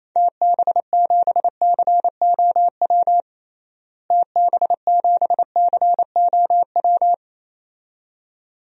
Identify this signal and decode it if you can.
Morse code, heard as a keyed tone, about 27 wpm, 700 Hz. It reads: TB7COW TB7COW
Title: DXCC-100 Call Signs - 6 Characters - Repeated 2x - 28wpm